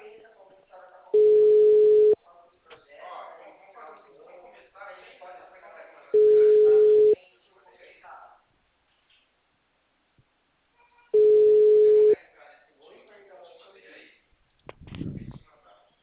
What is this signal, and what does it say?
Ringback tone (Europe, 425 Hz), with speech.